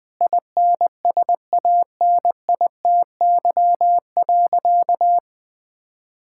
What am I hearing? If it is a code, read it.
Morse code, heard as a keyed tone, about 20 wpm, 700 Hz. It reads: INSANITY.
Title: Sentences with 8-Word Phrases - 20wpm